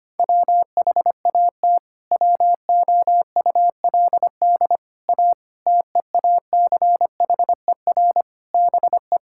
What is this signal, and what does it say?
Morse code, heard as a keyed tone, about 25 wpm, 700 Hz. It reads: WHAT WOULD A TEACHER BE
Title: Sentences from Top 1000 Words - Review Part 6 - 25wpm